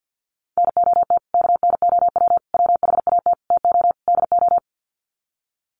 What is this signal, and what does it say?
Morse code, heard as a keyed tone, about 48 wpm, 700 Hz. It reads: NOT KNOW WHAT TO DO
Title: Sets of 5 Words 50wpm